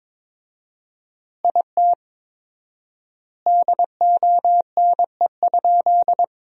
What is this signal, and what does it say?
Morse code, heard as a keyed tone, about 22 wpm, 700 Hz. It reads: IT DONE?
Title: 4x Word-Spacing - Sentences from Top 500 Words - 22wpm